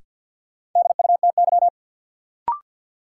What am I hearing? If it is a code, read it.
Morse code, heard as a keyed tone, about 52 wpm, 700 Hz. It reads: DUTY
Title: Sentences from Top 1500 Words - New Word Drill Alt - 50wpm